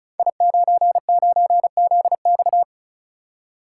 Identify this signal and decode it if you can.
Morse code, heard as a keyed tone, about 36 wpm, 700 Hz. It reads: I99ZX
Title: Code Groups - 5 Letter:Numbers - 35wpm